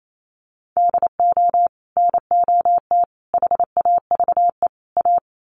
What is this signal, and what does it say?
Morse code, heard as a keyed tone, about 28 wpm, 700 Hz. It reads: DO NOT HAVE A
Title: Sets of 4 Words 28wpm